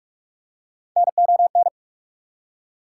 Morse code, heard as a keyed tone, about 45 wpm, 690 Hz. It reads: NON
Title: Prefixes and Suffixes 45wpm